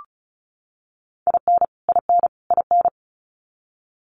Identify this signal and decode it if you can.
Morse code, heard as a keyed tone, about 34 wpm, 700 Hz. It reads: IN IN IN
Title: US States Repeated 3x 35wpm